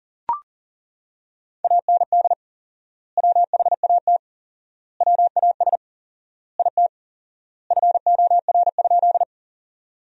Morse code, heard as a keyed tone, about 37 wpm, 700 Hz. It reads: AND WHAT WAS IT FOR?
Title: 4x Word-Spacing - Sentences from Top 100 Words - 40wpm